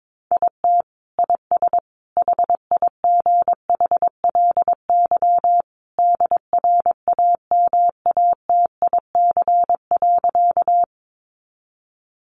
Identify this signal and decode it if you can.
Morse code, heard as a keyed tone, about 22 wpm, 700 Hz. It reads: IT IS HIGHLY DRAMATIC.
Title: Sentences from Top 1300 Words 22wpm